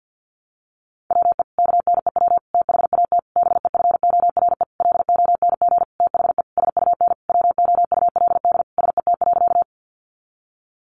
Morse code, heard as a keyed tone, about 50 wpm, 700 Hz. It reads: WE KNEW THAT BEFORE LONG THE SUN WOULD SET.
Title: Sentences from Top 400 Words 50wpm